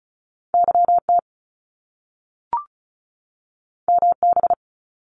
Morse code, heard as a keyed tone, about 36 wpm, 700 Hz. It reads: YT MB
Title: Canadian Provinces and Territories 35wpm